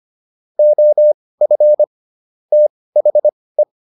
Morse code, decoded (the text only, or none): OF THE